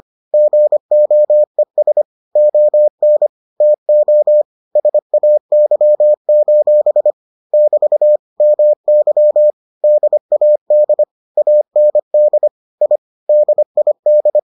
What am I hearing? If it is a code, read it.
Morse code, heard as a keyed tone, about 25 wpm, 600 Hz. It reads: GOES ON TO SAY: <BT> MY DAD AND I DID